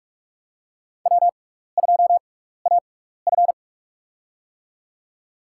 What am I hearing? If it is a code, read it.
Morse code, heard as a keyed tone, about 46 wpm, spaced as for 15 wpm, 700 Hz. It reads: W2AF
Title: USA Call Signs - 4 Characters - 6x Character Spacing - 45wpm